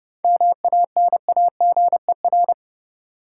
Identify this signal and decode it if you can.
Morse code, heard as a keyed tone, about 32 wpm, 700 Hz. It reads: MANAGER
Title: Seven Letter Words - 30wpm